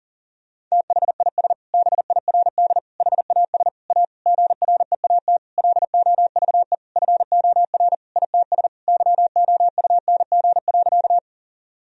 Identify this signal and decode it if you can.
Morse code, heard as a keyed tone, about 40 wpm, 700 Hz. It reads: THIS BIRD HAS A GREAT LOVE FOR ITS YOUNG.
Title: Sentences from Top 1000 Words - Review Part 2 - 40wpm